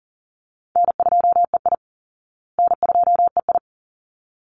Morse code, heard as a keyed tone, about 40 wpm, 700 Hz. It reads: N2EI N2EI
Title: Easy Call Signs Repeated 2x - 40wpm